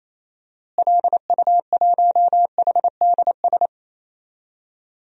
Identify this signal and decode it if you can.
Morse code, heard as a keyed tone, about 27 wpm, 700 Hz. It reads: LU1HDS
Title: DXCC-100 Call Signs - 6 Characters - Repeated 3x - 28wpm